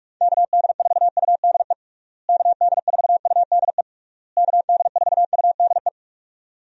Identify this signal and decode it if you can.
Morse code, heard as a keyed tone, about 46 wpm, 700 Hz. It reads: KD4UDE KD4UDE KD4UDE
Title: USA Call Signs - 6 Characters - Repeated 3x - 45wpm